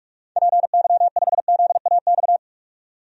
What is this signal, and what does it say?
Morse code, heard as a keyed tone, about 46 wpm, 700 Hz. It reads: PY5ZAX
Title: DXCC-100 Call Signs - 6 Characters - 45wpm